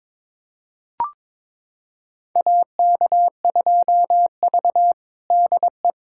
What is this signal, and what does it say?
Morse code, heard as a keyed tone, about 22 wpm, 700 Hz. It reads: AK2V DE
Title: USA Callsign DE Callsign - 4 Characters - 22wpm